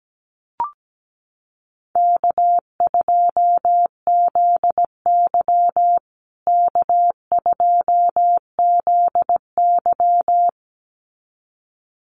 Morse code, heard as a keyed tone, about 17 wpm, 700 Hz. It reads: K2ZY K2ZY
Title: USA Call Signs Repeated 2x 17wpm